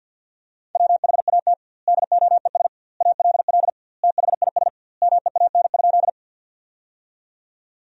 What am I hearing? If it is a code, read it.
Morse code, heard as a keyed tone, about 46 wpm, 700 Hz. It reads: WHAT DOES ALL THIS MEAN?
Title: 2x Word-Spacing - Sentences from Top 200 Words - 50wpm